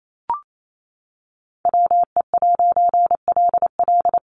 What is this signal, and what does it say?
Morse code, heard as a keyed tone, about 28 wpm, 700 Hz. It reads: WE'LL
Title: A Sound Of Thunder 28wpm